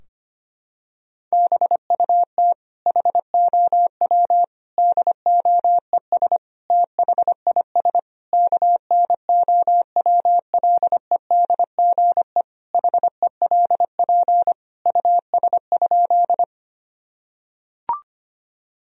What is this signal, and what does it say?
Morse code, heard as a keyed tone, about 25 wpm, 700 Hz. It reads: BUT HOW DOES THIS KNOWLEDGE HELP US?
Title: Sentences from Top 1000 Words - Review Part 3 - 25wpm